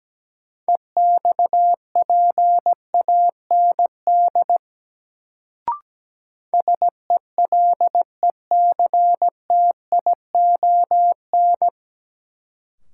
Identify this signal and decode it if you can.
Morse code, heard as a keyed tone, about 17 wpm, 700 Hz. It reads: EXPAND SELECTION
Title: Sentences from Top 1400 Words - New Word Drill - 17wpm